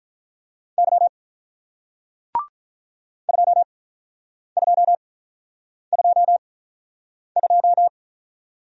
Morse code, heard as a keyed tone, about 48 wpm, spaced as for 9 wpm, 700 Hz. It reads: Y 2222